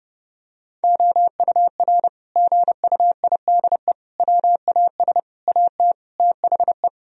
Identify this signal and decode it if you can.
Morse code, heard as a keyed tone, about 30 wpm, 700 Hz. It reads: OUR GUIDE WAS AT THE
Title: Sentences from Top 1000 Words - Review Part 3 - 30wpm